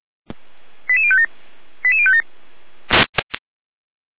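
Five-tone selective call (ZVEI), heard as quick runs of five tones, tones of about 70 ms. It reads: 88057 88057
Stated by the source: ZVEI five-tone call 88197 received by SDR, channel 455.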